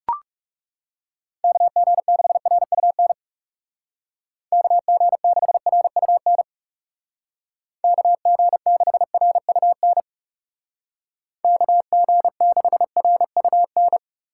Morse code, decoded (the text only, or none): KG6RUN KG6RUN KG6RUN KG6RUN